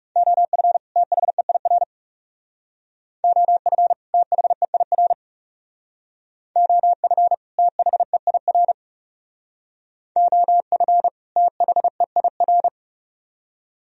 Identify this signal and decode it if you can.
Morse code, heard as a keyed tone, about 38 wpm, 700 Hz. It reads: OF THEIR OF THEIR OF THEIR OF THEIR